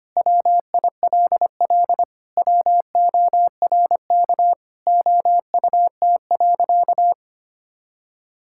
Morse code, heard as a keyed tone, about 25 wpm, 700 Hz. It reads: WILL WORK OUT.
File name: practice / 25wpm / Sentences with 3-Word Phrases - 25wpm